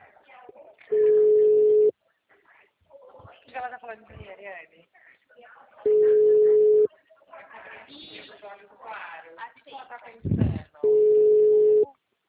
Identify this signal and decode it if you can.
Ringback tone (Europe, 425 Hz), with speech.
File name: out- -2010- - - .496